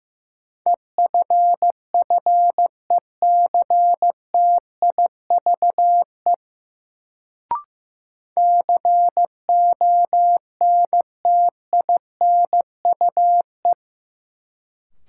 Morse code, heard as a keyed tone, about 15 wpm, 700 Hz. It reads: EFFECTIVE CONTINUE
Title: Sentences from Top 1000 Words - Review Part 6 - New Word Drill - 15wpm